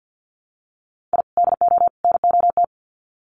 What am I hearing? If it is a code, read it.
Morse code, heard as a keyed tone, about 50 wpm, 700 Hz. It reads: I DO NOT
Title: Sets of 3 Words 50wpm